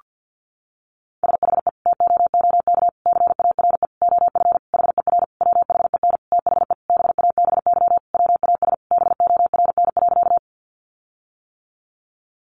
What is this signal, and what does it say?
Morse code, heard as a keyed tone, about 50 wpm, 700 Hz. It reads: SHE TOOK CARE OF HER WHEN THE BABY WAS BORN.
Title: Sentences from Top 1300 Words 50wpm